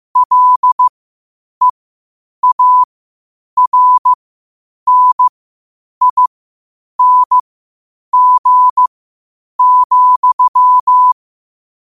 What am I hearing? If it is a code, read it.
Morse code, heard as a keyed tone, about 15 wpm, spaced as for 8 wpm, 1000 Hz. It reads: LEARNING,
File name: qotdfiles / 8wpm